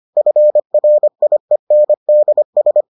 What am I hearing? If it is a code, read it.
Morse code, heard as a keyed tone, about 25 wpm, 600 Hz. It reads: FRIENDS